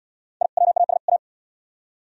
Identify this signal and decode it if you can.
Morse code, heard as a keyed tone, about 77 wpm, 700 Hz. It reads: I WAS A